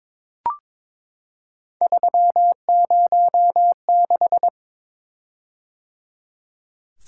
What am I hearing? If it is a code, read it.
Morse code, heard as a keyed tone, about 22 wpm, 700 Hz. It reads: 306
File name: practice / 22wpm / Three Letter-Number Combinations - 22wpm